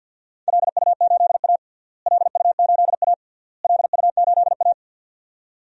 Morse code, heard as a keyed tone, about 45 wpm, 700 Hz. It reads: LU8A LU8A LU8A
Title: DXCC-100 Call Signs - 4 Characters - Repeated 3x - 50wpm